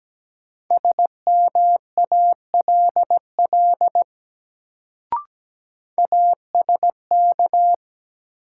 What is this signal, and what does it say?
Morse code, heard as a keyed tone, about 17 wpm, 700 Hz. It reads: SMALL ASK
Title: Sentences from Top 200 Words - New Word Drill - 17wpm